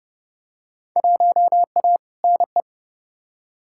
Morse code, heard as a keyed tone, about 32 wpm, 700 Hz. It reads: 1A NE
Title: Field Day Exchanges Encore - 30wpm